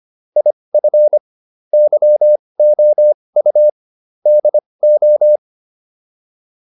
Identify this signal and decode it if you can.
Morse code, heard as a keyed tone, about 25 wpm, spaced as for 20 wpm, 600 Hz. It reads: IF YOU DO
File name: WD202 20